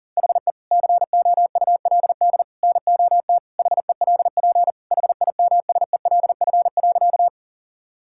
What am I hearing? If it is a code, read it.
Morse code, heard as a keyed tone, about 40 wpm, 700 Hz. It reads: HE COULD NOT HELP HIMSELF.
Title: Sentences from Top 1600 Words 40wpm